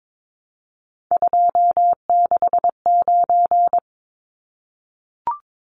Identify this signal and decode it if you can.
Morse code, heard as a keyed tone, about 22 wpm, 700 Hz. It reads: 269
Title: Numbers - 3-Digit 22wpm